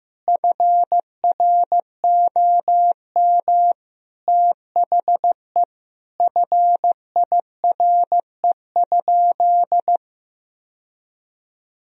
Morse code, heard as a keyed tone, about 16 wpm, 700 Hz. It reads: FROM THE FIRE?
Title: Sentences from Top 1000 Words - Review Part 3 - 15wpm